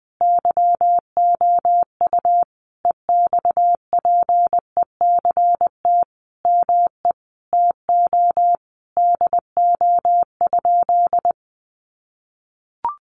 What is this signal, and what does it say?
Morse code, heard as a keyed tone, about 20 wpm, 700 Hz. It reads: YOU EXPECT ME TO DO?
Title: Sentences from Top 1100 Words 20wpm